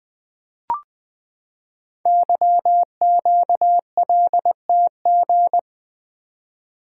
Morse code, heard as a keyed tone, about 20 wpm, 700 Hz. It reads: YQLTG
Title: Code Groups - 5 Letters - 20wpm